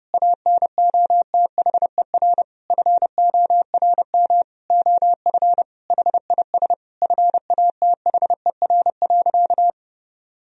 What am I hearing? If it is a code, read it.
Morse code, heard as a keyed tone, about 32 wpm, 700 Hz. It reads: ANOTHER FORM OF HIS FATHER.
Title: Sentences from Top 1000 Words - Review Part 1 - 30wpm